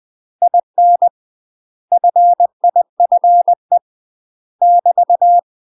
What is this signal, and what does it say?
Morse code, heard as a keyed tone, about 20 wpm, 700 Hz. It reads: IN FIFE <BT>
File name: UK MASTER QSOs TRAINING / 20WPM / UK MASTER QSOs 0057